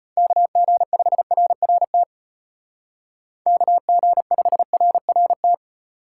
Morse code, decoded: KG5RRT KG5RRT

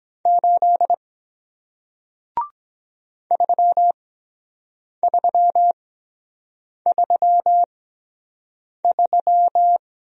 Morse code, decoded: 8 3 3 3 3

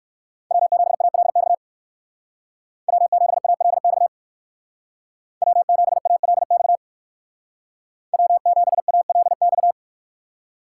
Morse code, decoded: W7ALX W7ALX W7ALX W7ALX

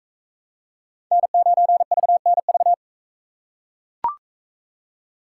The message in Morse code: N9VNV

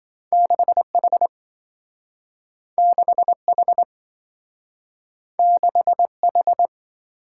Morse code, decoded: 6H 6H 6H

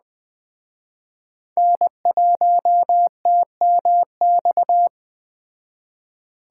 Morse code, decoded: N1TMX

